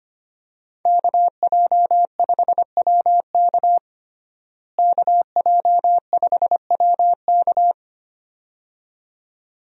Morse code, heard as a keyed tone, about 25 wpm, 700 Hz. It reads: KJ5WK KJ5WK
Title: USA Call Signs - 5 Characters - Repeated 2x - 25wpm